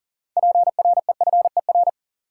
Morse code, heard as a keyed tone, about 40 wpm, 700 Hz. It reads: PREFER
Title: Sentences from Top 1100 Words - New Word Drill - 40wpm